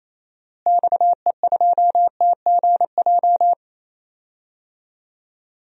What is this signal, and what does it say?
Morse code, heard as a keyed tone, about 28 wpm, 700 Hz. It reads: XE2TGJ
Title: DXCC-100 Call Signs - 6 Characters - Repeated 2x - 28wpm